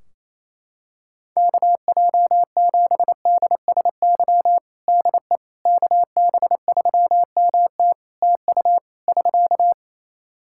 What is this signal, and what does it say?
Morse code, heard as a keyed tone, about 28 wpm, 700 Hz. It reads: KJ7DSY DE KB3MT TU <SK>
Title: K1USN SST QSO Practice 28wpm